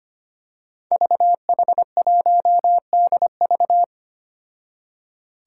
Morse code, heard as a keyed tone, about 25 wpm, 700 Hz. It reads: VH1DV
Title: Five Challenging Letter-Number Combinations 25wpm